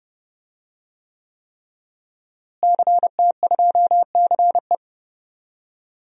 Morse code, heard as a keyed tone, about 30 wpm, 700 Hz. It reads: CT2CE
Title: DXCC-100 Call Signs - 5 Characters - Repeated 3x - 30wpm